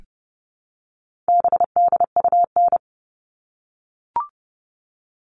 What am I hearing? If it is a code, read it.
Morse code, heard as a keyed tone, about 30 wpm, 700 Hz. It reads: BDUN
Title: All Cut Numbers - 4-Digit 30wpm